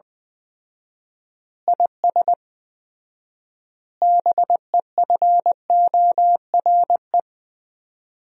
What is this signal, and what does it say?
Morse code, heard as a keyed tone, about 20 wpm, 700 Hz. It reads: IS BEFORE
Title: 4x Word-Spacing - Sentences from Top 200 Words - 20wpm